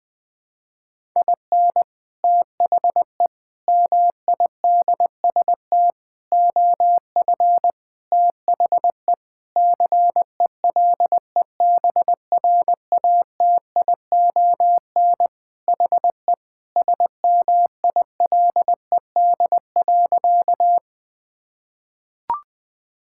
Morse code, heard as a keyed tone, about 21 wpm, 700 Hz. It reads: IN THE MIDST OF THE CELEBRATION HE SMILED.
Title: Sentences with 5-Word Phrases - 20wpm